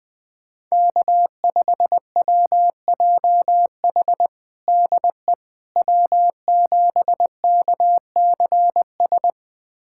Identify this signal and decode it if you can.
Morse code, heard as a keyed tone, about 20 wpm, 700 Hz. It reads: K5WJH DE W7KCS
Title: USA Callsign DE Callsign - 5 Characters - 20wpm